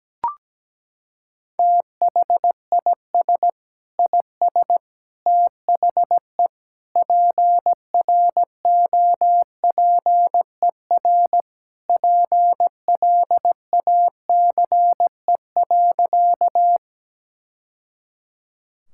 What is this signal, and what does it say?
Morse code, heard as a keyed tone, about 17 wpm, 700 Hz. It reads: THIS IS THE PROPER PLACE.
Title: Sentences from Top 1000 Words - Review Part 5 - 17wpm